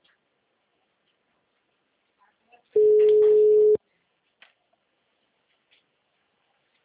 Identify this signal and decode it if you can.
Ringback tone (Europe, 425 Hz).
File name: out- -2037- - - .33530